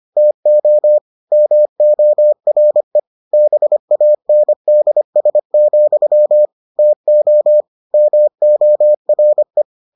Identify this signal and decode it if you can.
Morse code, heard as a keyed tone, about 25 wpm, 600 Hz. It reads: TO MORE BANDS, TO MORE